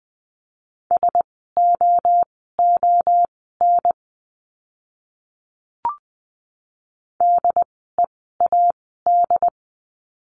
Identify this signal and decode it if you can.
Morse code, heard as a keyed tone, about 20 wpm, spaced as for 14 wpm, 700 Hz. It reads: SOON DEAD
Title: 075 Lesson 13f - Letter L - Words 20wpm